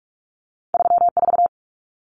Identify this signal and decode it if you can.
Morse code, heard as a keyed tone, about 46 wpm, 700 Hz. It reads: 34
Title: Common Cut Numbers - 2-Digit 45wpm